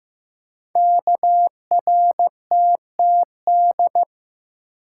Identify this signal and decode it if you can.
Morse code, heard as a keyed tone, about 15 wpm, 700 Hz. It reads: KRTTD
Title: Code Groups - 5 Letter:Numbers - 15wpm